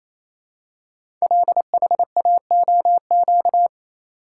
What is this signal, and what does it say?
Morse code, heard as a keyed tone, about 27 wpm, 700 Hz. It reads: LHAOQ
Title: Code Groups - 5 Letters - 28wpm